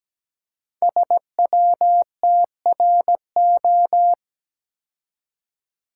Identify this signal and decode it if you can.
Morse code, heard as a keyed tone, about 17 wpm, 700 Hz. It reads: SWTRO
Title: Code Groups - 5 Letters - 17wpm